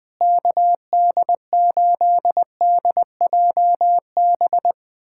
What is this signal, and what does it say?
Morse code, heard as a keyed tone, about 21 wpm, 700 Hz. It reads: KD8DJB
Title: USA Callsign DE Callsign 20wpm